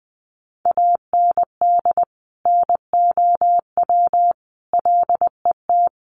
Morse code, heard as a keyed tone, about 20 wpm, 700 Hz. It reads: AND NOW LET
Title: Sentences from Top 1500 Words 20wpm